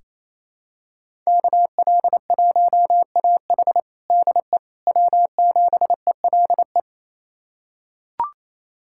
Morse code, heard as a keyed tone, about 28 wpm, 700 Hz. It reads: KL1AH DE W7ELE
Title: USA Callsign DE Callsign 28wpm